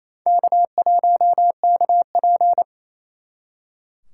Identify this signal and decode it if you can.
Morse code, heard as a keyed tone, about 37 wpm, 700 Hz. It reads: K1KP